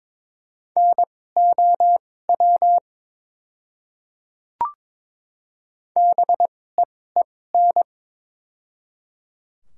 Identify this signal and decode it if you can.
Morse code, heard as a keyed tone, about 22 wpm, spaced as for 16 wpm, 700 Hz. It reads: NOW BEEN
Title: 2x Character Spacing - Top 100 Words - 22wpm